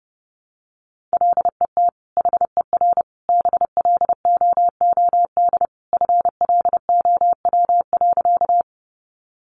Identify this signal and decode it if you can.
Morse code, heard as a keyed tone, about 30 wpm, 700 Hz. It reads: LET HER BLOOD FLOW.
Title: Sentences from Top 1200 Words 30wpm